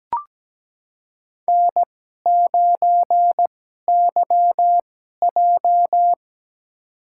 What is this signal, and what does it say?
Morse code, heard as a keyed tone, about 17 wpm, spaced as for 12 wpm, 700 Hz. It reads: N9YJ
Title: USA Call Signs - 4 Characters - 2x Character Spacing - 17wpm